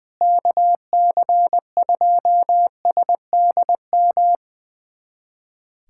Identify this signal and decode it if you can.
Morse code, heard as a keyed tone, about 21 wpm, 700 Hz. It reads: KC2SDM
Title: USA Callsign DE Callsign - 6 Characters - 20wpm